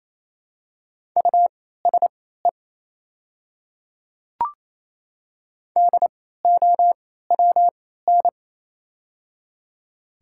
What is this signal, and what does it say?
Morse code, heard as a keyed tone, about 28 wpm, spaced as for 16 wpm, 700 Hz. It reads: USE DOWN
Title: 3x Character Spacing - Top 200 Words - 28wpm